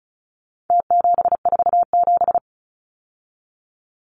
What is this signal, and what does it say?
Morse code, heard as a keyed tone, about 34 wpm, 700 Hz. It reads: T747
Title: Numbers - Contest Style - 4-Digit 35wpm